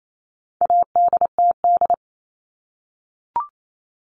Morse code, heard as a keyed tone, about 28 wpm, 700 Hz. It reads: ADTD
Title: All Cut Numbers - 4-Digit 28wpm